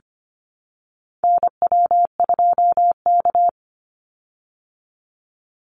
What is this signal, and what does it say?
Morse code, heard as a keyed tone, about 25 wpm, 700 Hz. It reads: NW2K
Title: Call Signs Repeated 3x - 25wpm